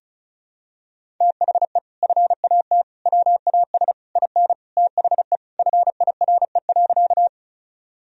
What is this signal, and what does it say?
Morse code, heard as a keyed tone, about 36 wpm, 700 Hz. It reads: THE FAT WAS IN THE FIRE.
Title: Sentences from Top 1000 Words - Review Part 4 - 35wpm